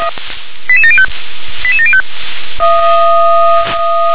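Five-tone selective call (ZVEI), heard as quick runs of five tones, tones of about 70 ms. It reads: 88705 88705
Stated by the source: ZVEI five-tone call 88791 received on an SDR.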